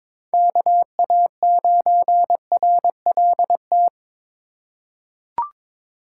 Morse code, heard as a keyed tone, about 22 wpm, 700 Hz. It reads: KA9RLT
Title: USA Call Signs - 6 Characters - 3x Character Spacing - 22wpm